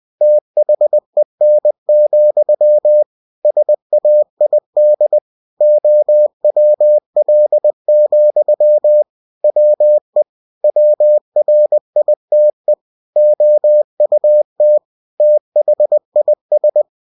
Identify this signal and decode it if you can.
Morse code, heard as a keyed tone, about 20 wpm, 600 Hz. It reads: THEN, SAID OWL, WE WRITE OUT THIS
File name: winnie-20wpm